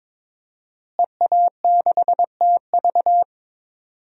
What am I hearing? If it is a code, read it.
Morse code, heard as a keyed tone, about 21 wpm, 700 Hz. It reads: EA6TV
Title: DXCC-100 Call Signs - 5 Characters - Repeated 2x - 22wpm